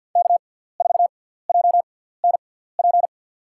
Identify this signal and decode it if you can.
Morse code, heard as a keyed tone, about 52 wpm, spaced as for 17 wpm, 700 Hz. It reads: K4JNP
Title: USA Call Signs - 6x Character Spacing - 50wpm